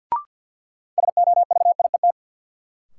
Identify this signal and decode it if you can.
Morse code, heard as a keyed tone, about 52 wpm, 700 Hz. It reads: SOVIET
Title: Sets of 5 Words Encore - New Word Drill - 50wpm